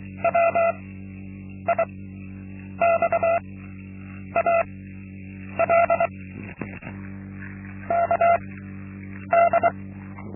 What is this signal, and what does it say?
Morse code, heard as a keyed tone, about 23 wpm, spaced as for 8 wpm, 680 Hz. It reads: WIXAL KD
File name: Linksmakalnis-2